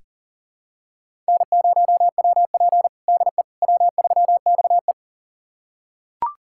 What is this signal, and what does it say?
Morse code, heard as a keyed tone, about 40 wpm, 690 Hz. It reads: N0WP DE W3XE
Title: USA Callsign DE Callsign - 4 Characters - 40wpm